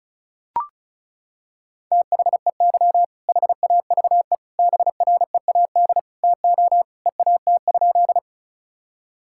Morse code, heard as a keyed tone, about 36 wpm, 700 Hz. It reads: THEY HAVE BREAD TO EAT?
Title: Sentences from Top 1600 Words 35wpm